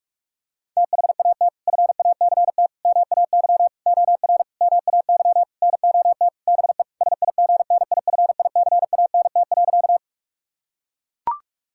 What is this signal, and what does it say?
Morse code, heard as a keyed tone, about 41 wpm, 700 Hz. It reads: THAT FACT MAY OR MAY NOT BE SIGNIFICANT.
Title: Sentences from Top 1000 Words - Review Part 6 - 45wpm